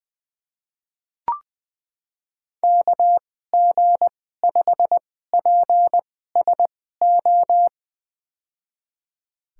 Morse code, heard as a keyed tone, about 20 wpm, spaced as for 15 wpm, 700 Hz. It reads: KG5PSO
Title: USA Call Signs - 6 Characters - 2x Character Spacing - 20wpm